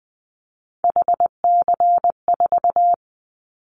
Morse code, heard as a keyed tone, about 20 wpm, 700 Hz. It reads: HC4
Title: Callsign Prefixes for Top 100 DXCC - 20wpm